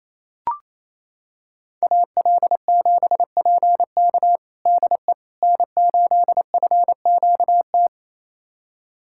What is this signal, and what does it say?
Morse code, heard as a keyed tone, about 28 wpm, 700 Hz. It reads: AL7PK DE N8FQT
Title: USA Callsign DE Callsign - 5 Characters - 28wpm